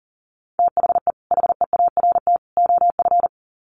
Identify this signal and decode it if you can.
Morse code, heard as a keyed tone, about 40 wpm, 700 Hz. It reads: THE HEART OF
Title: Sets of 3 Words 40wpm